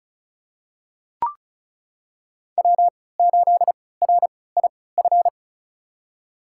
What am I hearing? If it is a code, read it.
Morse code, heard as a keyed tone, about 34 wpm, spaced as for 20 wpm, 700 Hz. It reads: W8RIF
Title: USA Call Signs - 5 Characters - 3x Character Spacing - 35wpm